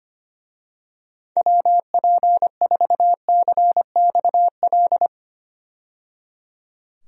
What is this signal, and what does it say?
Morse code, heard as a keyed tone, about 25 wpm, 700 Hz. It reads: WP4CXL
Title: USA Call Signs - 6 Characters - 25wpm